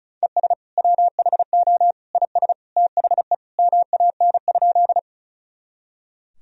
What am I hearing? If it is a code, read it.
Morse code, heard as a keyed tone, about 34 wpm, 690 Hz. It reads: ES WHO IS THE MAN?
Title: Sentences from Top 200 Words and Abbreviations 35wpm